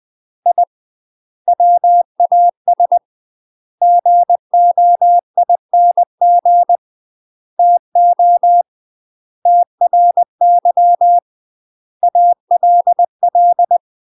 Morse code, heard as a keyed tone, about 20 wpm, 700 Hz. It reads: I WAS GOING TO TRY ALL
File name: UK MASTER QSOs TRAINING / 20WPM / UK MASTER QSOs 0070